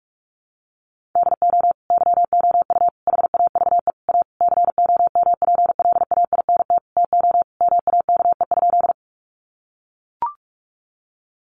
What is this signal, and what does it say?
Morse code, heard as a keyed tone, about 44 wpm, 700 Hz. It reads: DO YOU HAVE A COMPLAINT TO MAKE?
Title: Sentences from Top 1400 Words 45wpm